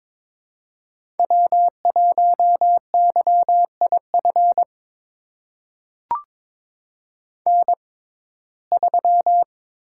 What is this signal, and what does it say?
Morse code, heard as a keyed tone, about 22 wpm, spaced as for 16 wpm, 700 Hz. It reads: W1YIF N3
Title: USA Call Signs - 5 Characters - 6x Character Spacing - 22wpm